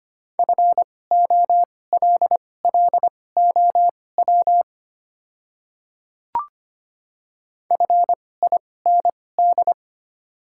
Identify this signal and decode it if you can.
Morse code, heard as a keyed tone, about 25 wpm, spaced as for 18 wpm, 690 Hz. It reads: FOLLOW FIND
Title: 2x Character Spacing - Top 200 Words - 25wpm